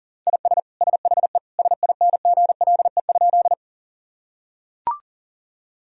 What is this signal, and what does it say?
Morse code, heard as a keyed tone, about 40 wpm, 700 Hz. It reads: IS SHE SINGLE?
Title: Sentences from Top 1000 Words - Review Part 4 - 40wpm